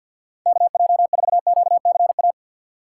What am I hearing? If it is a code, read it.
Morse code, heard as a keyed tone, about 53 wpm, 700 Hz. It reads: KJ4QCA
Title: USA Call Signs - 3x Character Spacing - 50wpm